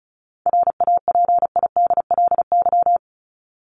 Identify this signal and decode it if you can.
Morse code, heard as a keyed tone, about 34 wpm, 700 Hz. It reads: RAPIDLY
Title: LY Suffix - Top 100 Words with Base Words 35wpm